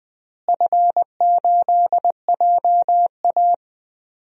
Morse code, heard as a keyed tone, about 20 wpm, 700 Hz. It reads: F8JA
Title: DXCC-100 Call Signs - 4 Characters - Repeated 3x - 20wpm